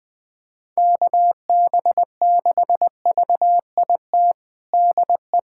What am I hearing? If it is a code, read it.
Morse code, heard as a keyed tone, about 20 wpm, 700 Hz. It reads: KB6VIT DE
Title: USA Callsign DE Callsign - 6 Characters - 20wpm